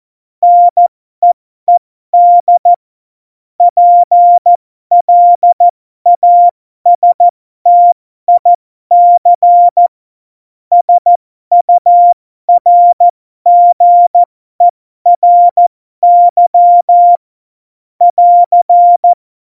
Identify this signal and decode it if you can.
Morse code, heard as a keyed tone, about 14 wpm, spaced as for 12 wpm, 700 Hz. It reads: NEED PLASTIC SURGERY <AR>
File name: Morsekurs / Witze